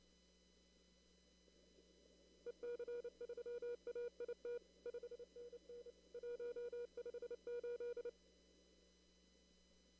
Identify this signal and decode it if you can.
Morse code, heard as a keyed tone, about 29 wpm, 480 Hz. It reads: EC3AIT 5NN 158
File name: CW / kt5j wpx cw-op-k5pi